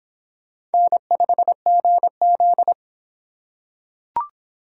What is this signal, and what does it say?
Morse code, heard as a keyed tone, about 25 wpm, 700 Hz. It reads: N5GZ